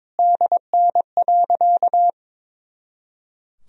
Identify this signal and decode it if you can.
Morse code, heard as a keyed tone, about 21 wpm, 700 Hz. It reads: DN.
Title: Sentences from Top 500 Words and Abbreviations 22wpm